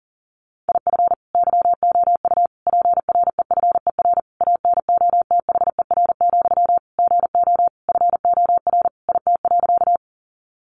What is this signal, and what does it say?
Morse code, heard as a keyed tone, about 40 wpm, 700 Hz. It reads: IF YOU PREFER ANOTHER, GO FOR IT.
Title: Sentences from Top 1100 Words 40wpm